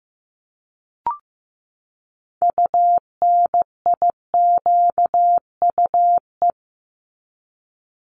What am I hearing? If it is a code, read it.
Morse code, heard as a keyed tone, about 15 wpm, 700 Hz. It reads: UNIQUE
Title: Top 751-1000 Words 15wpm